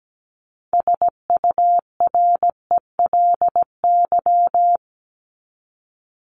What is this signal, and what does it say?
Morse code, heard as a keyed tone, about 17 wpm, 700 Hz. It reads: SURELY
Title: LY Suffix - Top 100 Words with Base Words 17wpm